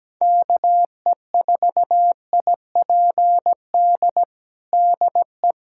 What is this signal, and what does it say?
Morse code, heard as a keyed tone, about 17 wpm, 700 Hz. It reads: KE4IPD DE
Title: USA Callsign DE Callsign 17wpm